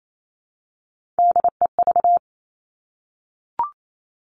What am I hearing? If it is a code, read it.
Morse code, heard as a keyed tone, about 28 wpm, 700 Hz. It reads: DEV
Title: Three Challenging Letter-Number Combinations 28wpm